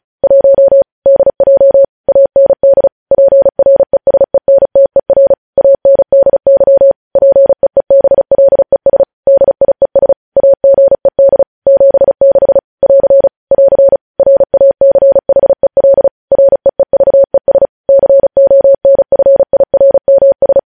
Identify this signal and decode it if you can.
Morse code, heard as a keyed tone, about 36 wpm, 550 Hz. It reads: 1 DJ AND PRESENTER ANDY PEEBLES DIES AGED 76 <AR> <AR> RACHEL REEVES CONFIRMS
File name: podcast / News headlines-1x-35x35